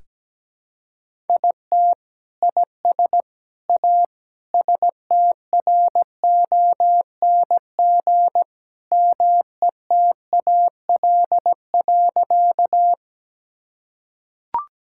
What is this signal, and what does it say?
Morse code, heard as a keyed tone, about 17 wpm, 700 Hz. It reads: IT IS A STRONG METAL.
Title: Sentences from Top 1000 Words - Review Part 6 - 17wpm